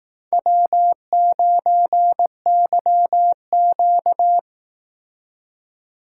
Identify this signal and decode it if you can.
Morse code, heard as a keyed tone, about 18 wpm, 700 Hz. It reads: W9YQ
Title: Speed Racing - USA 1x2 Call Signs 15wpm